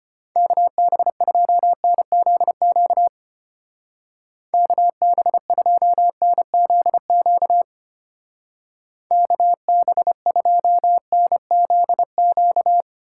Morse code, heard as a keyed tone, about 32 wpm, 700 Hz. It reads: KB2NZQ KB2NZQ KB2NZQ